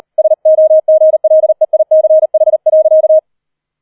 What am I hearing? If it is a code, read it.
Morse code, heard as a keyed tone, about 38 wpm, 630 Hz. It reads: SOGLEICH.
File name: 004 Vierter Streich 40wpm